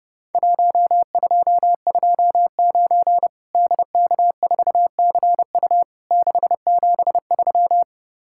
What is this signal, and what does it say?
Morse code, heard as a keyed tone, about 30 wpm, 700 Hz. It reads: 1229 DK4CU 673
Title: Worked All Europe DX Contest - QTC Exchange - 30wpm